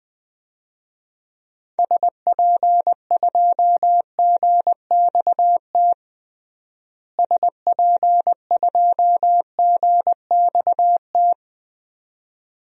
Morse code, heard as a keyed tone, about 21 wpm, 700 Hz. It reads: SP2GXT SP2GXT
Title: DXCC-100 Call Signs - 6 Characters - Repeated 2x - 20wpm